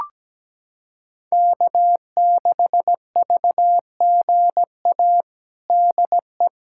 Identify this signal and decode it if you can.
Morse code, heard as a keyed tone, about 17 wpm, 700 Hz. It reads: K6VGA DE
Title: USA Callsign DE Callsign 17wpm